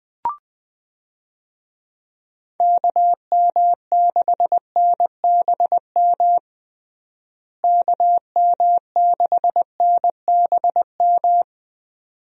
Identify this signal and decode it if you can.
Morse code, heard as a keyed tone, about 21 wpm, 700 Hz. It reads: KM6NBM KM6NBM
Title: USA Call Signs - 6 Characters - Repeated 2x - 20wpm